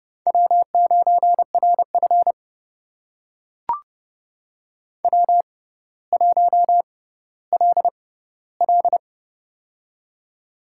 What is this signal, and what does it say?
Morse code, heard as a keyed tone, about 32 wpm, spaced as for 16 wpm, 700 Hz. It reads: W9RF W1LL
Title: USA Call Signs - 4 Characters - 6x Character Spacing - 30wpm